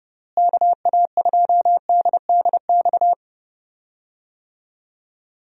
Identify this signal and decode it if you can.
Morse code, heard as a keyed tone, about 32 wpm, 700 Hz. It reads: KA2DDX
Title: USA Call Signs - 6 Characters - Repeated 2x - 30wpm